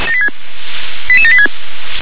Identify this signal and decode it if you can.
Five-tone selective call (ZVEI), heard as quick runs of five tones, tones of about 80 ms. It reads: 88786